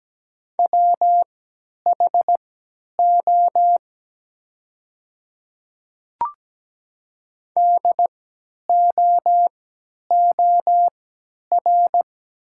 Morse code, heard as a keyed tone, about 17 wpm, spaced as for 10 wpm, 690 Hz. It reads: WHO DOOR
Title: 3x Character Spacing - Top 300 Words - 17wpm